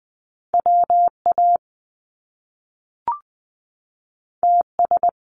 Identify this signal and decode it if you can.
Morse code, heard as a keyed tone, about 19 wpm, 700 Hz. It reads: WA TS
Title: Top Two Letter Combinations in Words 20wpm